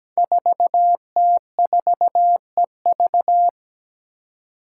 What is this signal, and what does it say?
Morse code, heard as a keyed tone, about 17 wpm, 700 Hz. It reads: 4T4EV
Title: Five Challenging Letter-Number Combinations 17wpm